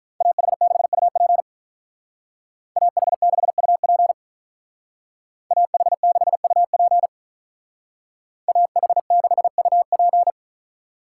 Morse code, heard as a keyed tone, about 44 wpm, 700 Hz. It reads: AH6UP AH6UP AH6UP AH6UP